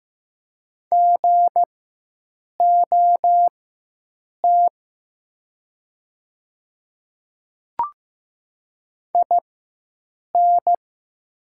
Morse code, heard as a keyed tone, about 15 wpm, spaced as for 7 wpm, 700 Hz. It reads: GOT IN
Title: 4x Character Spacing - Top 300 Words - 15wpm